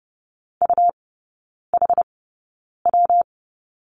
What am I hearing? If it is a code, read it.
Morse code, heard as a keyed tone, about 30 wpm, spaced as for 9 wpm, 700 Hz. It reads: UHW
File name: Lesson 19e - Letter w - Three Characters 30wpm